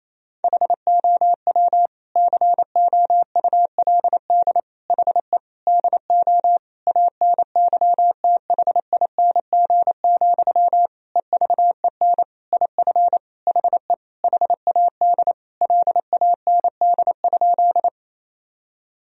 Morse code, heard as a keyed tone, about 28 wpm, 700 Hz. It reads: HOW COULD HE DO ANYTHING, EVEN IF HE HAD LAND?
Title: Sentences from Top 1000 Words - Review Part 1 - 28wpm